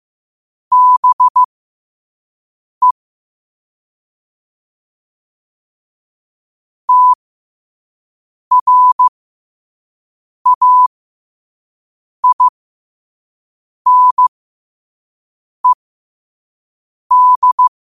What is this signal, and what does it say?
Morse code, heard as a keyed tone, about 15 wpm, spaced as for 5 wpm, 1000 Hz. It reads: BE TRAINED
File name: qotdfiles / 5wpm